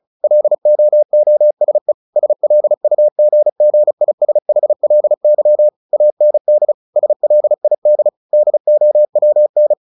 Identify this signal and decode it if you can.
Morse code, heard as a keyed tone, about 34 wpm, 600 Hz. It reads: LOOSE SLUGGISHLY AND SLID DOWN
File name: the old man and the sea 2 35wpm